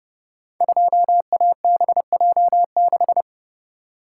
Morse code, heard as a keyed tone, about 30 wpm, 700 Hz. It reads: 2ABJ6
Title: Code Groups - 5 Letter:Numbers - 30wpm